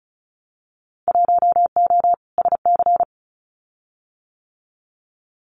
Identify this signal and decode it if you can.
Morse code, heard as a keyed tone, about 36 wpm, 700 Hz. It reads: 1O SC
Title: Winter Field Day Exchanges 35wpm